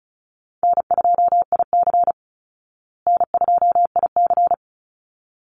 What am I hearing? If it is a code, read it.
Morse code, heard as a keyed tone, about 34 wpm, 700 Hz. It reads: N2IC N2IC
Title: Easy Call Signs Repeated 2x - 35wpm